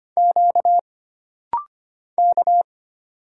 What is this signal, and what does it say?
Morse code, heard as a keyed tone, about 25 wpm, 700 Hz. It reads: Q K
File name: Single Character (Letter:Number:Punctuation:Prosign) - Rapid-Fire - 25wpm